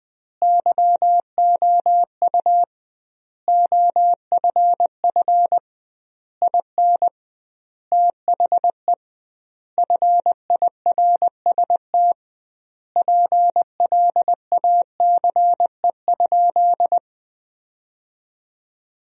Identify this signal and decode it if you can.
Morse code, heard as a keyed tone, about 20 wpm, 700 Hz. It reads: YOU OFF IN THE FIRST PLACE?
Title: 2x Word-Spacing - Sentences from Top 200 Words - 20wpm